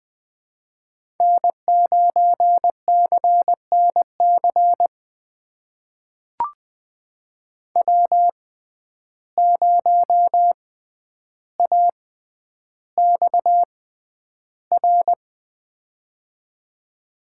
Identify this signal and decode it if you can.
Morse code, heard as a keyed tone, about 20 wpm, spaced as for 10 wpm, 700 Hz. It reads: N9CNC W0AXR
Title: USA Call Signs - 5 Characters - 6x Character Spacing - 20wpm